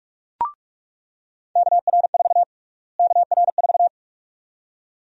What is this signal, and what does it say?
Morse code, heard as a keyed tone, about 45 wpm, 700 Hz. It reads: KR4 KR4
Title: USA Call Sign Prefixes - Repeated 2x - 45wpm